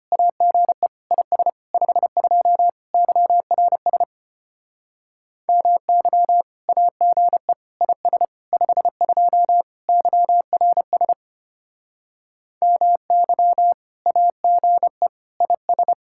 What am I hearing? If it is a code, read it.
Morse code, heard as a keyed tone, about 32 wpm, 700 Hz. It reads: AGE IS 52 YRS MY AGE IS 52 YRS MY AGE IS